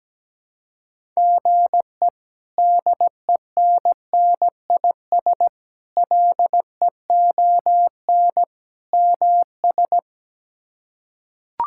Morse code, heard as a keyed tone, about 17 wpm, 700 Hz. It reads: GE DENNIS LEON MS
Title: K1USN SST QSO Practice - 17wpm